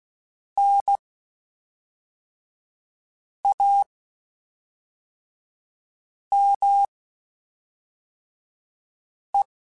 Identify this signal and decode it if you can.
Morse code, heard as a keyed tone, about 16 wpm, spaced as for 3 wpm, 800 Hz. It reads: NAME
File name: (spaced) / QSOs Spaced-mp3s / kc7haa @5 sp